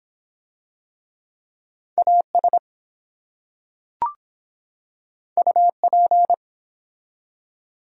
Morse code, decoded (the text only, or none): AS UP